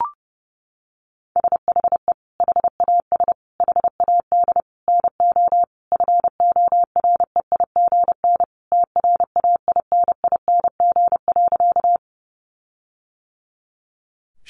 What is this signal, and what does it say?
Morse code, heard as a keyed tone, about 30 wpm, 700 Hz. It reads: SHE HAS HAD NO FOREIGN TRAINING.
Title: Sentences from Top 1400 Words 30wpm